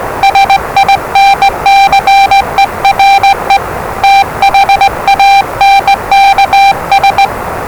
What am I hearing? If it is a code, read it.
Morse code, heard as a keyed tone, about 20 wpm, 790 Hz, with noise. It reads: SINCERE THANKS
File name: WellingtonRadioZlwCloseDownOnHfAndMf / 01-Zlw 4277khz 417.5 qrt until 0925z